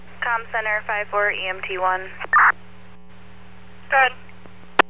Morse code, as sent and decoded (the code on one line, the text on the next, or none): none